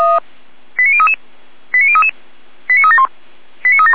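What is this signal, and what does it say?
Five-tone selective call (ZVEI), heard as quick runs of five tones, tones of about 70 ms. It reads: 89033 89033 89371 89371
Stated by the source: ZVEI five-tone selective call received on SDR.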